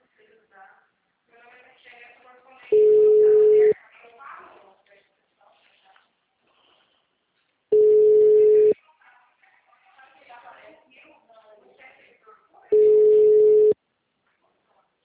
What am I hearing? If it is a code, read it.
Ringback tone (Europe, 425 Hz), with speech.
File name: out- -2024- - - .22959